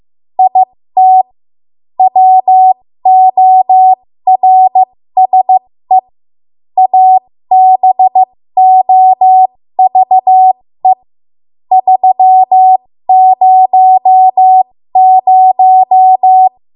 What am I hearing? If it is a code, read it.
Morse code, heard as a keyed tone, about 15 wpm, spaced as for 13 wpm, 750 Hz. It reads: IT WORSE ABOVE 300
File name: Morse / 13WPM